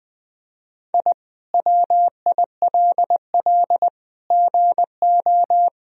Morse code, heard as a keyed tone, about 20 wpm, 700 Hz. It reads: I WILL GO
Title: Sentences from Top 1000 Words - Review Part 3 - 20wpm